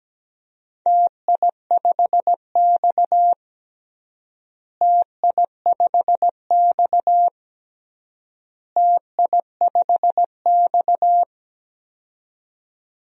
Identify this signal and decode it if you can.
Morse code, heard as a keyed tone, about 17 wpm, 700 Hz. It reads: TI5X TI5X TI5X
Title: DXCC-100 Call Signs - 4 Characters - Repeated 3x - 17wpm